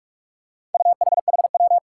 Morse code, heard as a keyed tone, about 44 wpm, 700 Hz. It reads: UHHW
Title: Four Challenging Letter-Number Combinations 45wpm